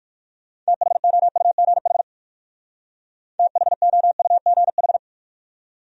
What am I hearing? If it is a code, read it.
Morse code, heard as a keyed tone, about 63 wpm, 700 Hz. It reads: THOUGH THOUGH